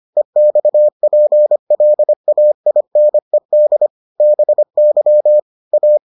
Morse code, heard as a keyed tone, about 25 wpm, 600 Hz. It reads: EXPLAINED BY A